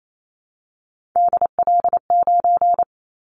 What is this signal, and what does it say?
Morse code, heard as a keyed tone, about 28 wpm, 700 Hz. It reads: DL9
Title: Callsign Prefixes for Top 25 DXCC - Repeated 3x - 28wpm